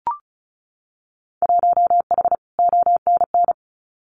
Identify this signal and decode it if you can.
Morse code, heard as a keyed tone, about 34 wpm, 700 Hz. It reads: 1H ONN
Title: Winter Field Day Exchanges 35wpm